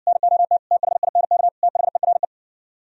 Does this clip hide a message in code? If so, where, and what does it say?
Yes: Morse code from 0.1 to 2.2 s, heard as a keyed tone, about 69 wpm, 700 Hz. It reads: NOT THINK THERE